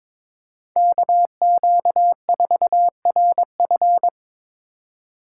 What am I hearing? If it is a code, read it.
Morse code, heard as a keyed tone, about 23 wpm, 700 Hz. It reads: KQ4RF
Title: USA Call Signs - 4x Character Spacing - 22wpm